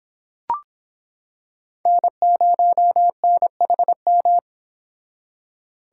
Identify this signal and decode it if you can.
Morse code, heard as a keyed tone, about 26 wpm, 700 Hz. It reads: N0NHM